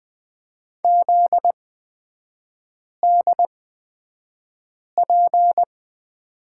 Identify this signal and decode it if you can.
Morse code, heard as a keyed tone, about 20 wpm, spaced as for 5 wpm, 700 Hz. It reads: ZDP
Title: 243 Lesson 39e - Letter z - Three Characters 20wpm